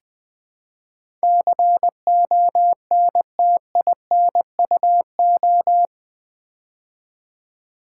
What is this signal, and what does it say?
Morse code, heard as a keyed tone, about 20 wpm, 700 Hz. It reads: CONTINUO
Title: Build A Word - 4 Letters At A Time - 10-Letter Words - 20wpm